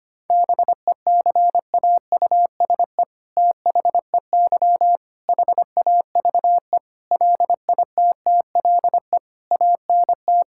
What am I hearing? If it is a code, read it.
Morse code, heard as a keyed tone, about 25 wpm, 700 Hz. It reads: BECAUSE THEY HAVE LITTLE ANT
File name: practice / 25wpm / All-Age Appropriate Jokes - 25wpm